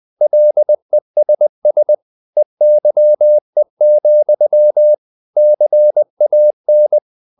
Morse code, heard as a keyed tone, about 20 wpm, 600 Hz. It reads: LESS EYE, CAN
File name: the old man and the sea / the old man and the sea part 1 20wpm